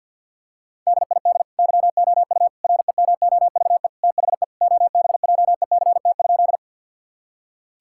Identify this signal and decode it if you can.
Morse code, heard as a keyed tone, about 51 wpm, 700 Hz. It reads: DID YOU REMOVE THE OBJECT?
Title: Sentences from Top 1000 Words - Review Part 5 - 50wpm